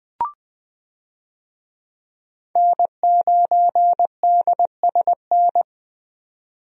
Morse code, heard as a keyed tone, about 20 wpm, 700 Hz. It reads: N9DSN
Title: USA Call Signs - 5 Characters - 20wpm